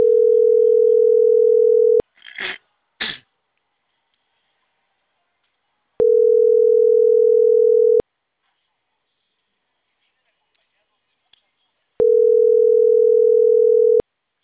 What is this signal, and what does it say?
Ringback tone (North America), 440 + 480 Hz.